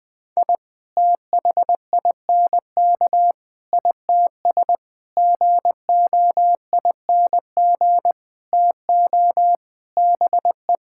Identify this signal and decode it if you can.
Morse code, heard as a keyed tone, about 21 wpm, 700 Hz. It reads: I THINK ITS GOING TO BE
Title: Sentences with 7-Word Phrases - Spoken Beginning - 20wpm